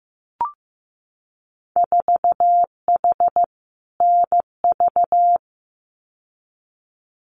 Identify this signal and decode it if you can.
Morse code, heard as a keyed tone, about 15 wpm, 700 Hz. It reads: 4H NV
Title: Winter Field Day Exchanges Encore 15wpm